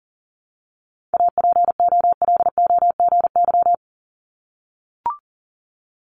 Morse code, heard as a keyed tone, about 40 wpm, 700 Hz. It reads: APOLOGY
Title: Multi-letter Phonemes Repeated 3x 40wpm